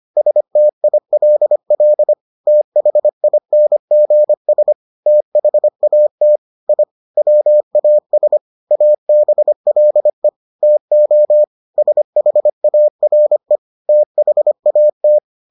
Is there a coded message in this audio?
Yes: Morse code from 0.2 to 15.2 s, heard as a keyed tone, about 25 wpm, 600 Hz. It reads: STILL THINGS THAT I WAS ABLE TO SHARE THAT